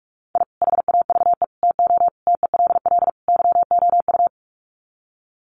Morse code, heard as a keyed tone, about 44 wpm, 700 Hz. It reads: I HAVE TO TELL YOU
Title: Sets of 5 Words 45wpm